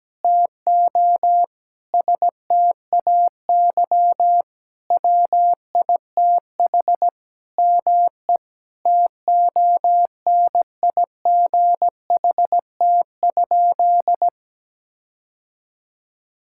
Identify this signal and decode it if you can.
Morse code, heard as a keyed tone, about 17 wpm, 700 Hz. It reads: TO STAY WITH ME TONIGHT?
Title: Sentences from Top 1600 Words 17wpm